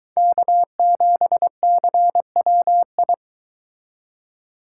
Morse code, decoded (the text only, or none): K7CWI